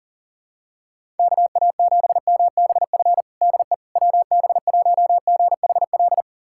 Morse code, heard as a keyed tone, about 37 wpm, 700 Hz. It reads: KA7MBF DE WB1GHL
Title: USA Callsign DE Callsign - 6 Characters - 40wpm